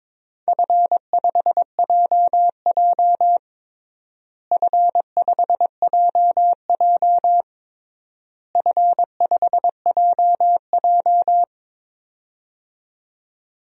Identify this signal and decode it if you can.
Morse code, heard as a keyed tone, about 22 wpm, 700 Hz. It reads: F5JJ F5JJ F5JJ
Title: DXCC-100 Call Signs - 4 Characters - Repeated 3x - 22wpm